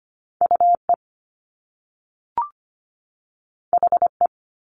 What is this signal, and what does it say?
Morse code, heard as a keyed tone, about 25 wpm, 700 Hz. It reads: UE HE
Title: Top Two Letter Combinations in Words 25wpm